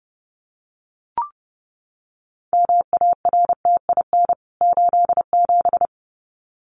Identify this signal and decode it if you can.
Morse code, heard as a keyed tone, about 30 wpm, 700 Hz. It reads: MARTIN 87
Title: CWT Exchange 30wpm